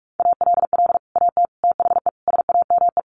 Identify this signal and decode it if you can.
Morse code, heard as a keyed tone, about 44 wpm, 700 Hz. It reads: ALL AT THE SAME
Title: Sets of 5 Words Encore2 45wpm